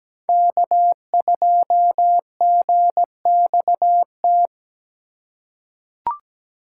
Morse code, heard as a keyed tone, about 17 wpm, 700 Hz. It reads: K2GXT
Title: USA Call Signs - 5 Characters - 4x Character Spacing - 17wpm